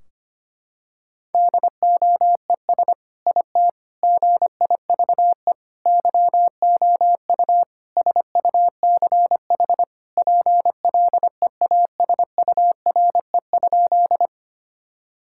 Morse code, heard as a keyed tone, about 25 wpm, 700 Hz. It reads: DOES IT GIVE YOU SUCH PLEASURE?
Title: Sentences from Top 1600 Words 25wpm